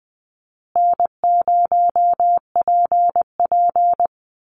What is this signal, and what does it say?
Morse code, heard as a keyed tone, about 20 wpm, 700 Hz. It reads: N0PP
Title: Easy Call Signs Repeated 2x - 20wpm